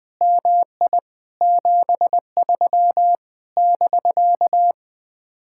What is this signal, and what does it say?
Morse code, heard as a keyed tone, about 20 wpm, 700 Hz. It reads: MI 73 <BK>